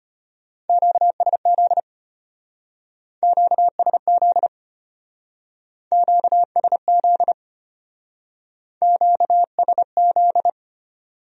Morse code, heard as a keyed tone, about 31 wpm, 700 Hz. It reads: QSZ QSZ QSZ QSZ
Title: Speed-Racing Q-Codes - 25wpm